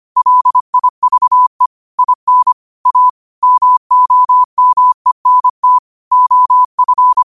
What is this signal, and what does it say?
Morse code, heard as a keyed tone, about 25 wpm, 1000 Hz. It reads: LIVE IN A MOMENT OF
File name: qotdfiles / 25wpm